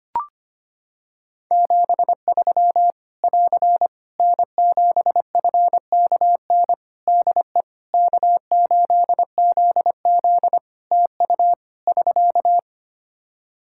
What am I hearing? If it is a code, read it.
Morse code, heard as a keyed tone, about 25 wpm, 700 Hz. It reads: 73 <AR> N7FKN DE K8ZZ TU <SK>
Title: Practice QSO - Exchange 4 - Ending - 25wpm